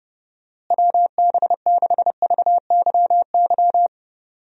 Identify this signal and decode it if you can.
Morse code, heard as a keyed tone, about 30 wpm, 700 Hz. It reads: WB6VYY
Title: USA Call Signs - 6 Characters - Repeated 2x - 30wpm